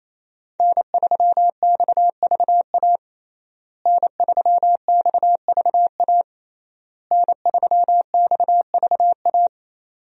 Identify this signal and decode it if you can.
Morse code, heard as a keyed tone, about 28 wpm, 700 Hz. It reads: N3XVA N3XVA N3XVA
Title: USA Call Signs - 5 Characters - Repeated 3x - 28wpm